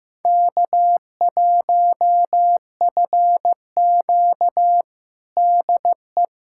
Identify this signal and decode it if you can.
Morse code, heard as a keyed tone, about 15 wpm, 700 Hz. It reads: K1FQ DE
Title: USA Callsign DE Callsign - 4 Characters - 15wpm